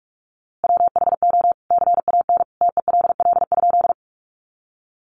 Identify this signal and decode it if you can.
Morse code, heard as a keyed tone, about 45 wpm, 700 Hz. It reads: WHO CAN TELL?
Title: A Sound Of Thunder 45wpm